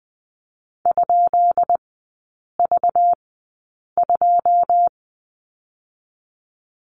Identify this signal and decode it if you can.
Morse code, heard as a keyed tone, about 20 wpm, 700 Hz. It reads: ? V 2
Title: 218 Lesson 35e - Number 0 - Three Characters 20wpm